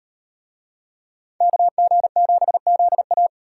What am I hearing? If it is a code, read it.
Morse code, heard as a keyed tone, about 38 wpm, 700 Hz. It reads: KG7ZA